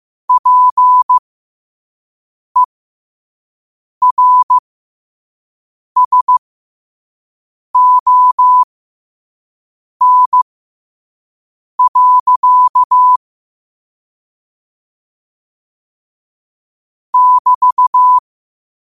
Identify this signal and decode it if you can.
Morse code, heard as a keyed tone, about 15 wpm, spaced as for 5 wpm, 1000 Hz. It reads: PERSON. <BT>
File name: qotdfiles / 5wpm